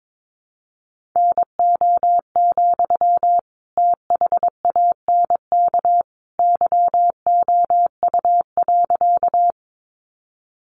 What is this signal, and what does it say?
Morse code, heard as a keyed tone, about 22 wpm, 700 Hz. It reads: NO, THANK YOU.
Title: Sentences from Top 600 Words 22wpm